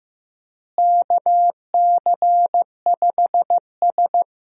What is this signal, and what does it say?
Morse code, heard as a keyed tone, about 15 wpm, 700 Hz. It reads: KC5S
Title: USA Call Signs - 4 Characters - Repeated 2x - 15wpm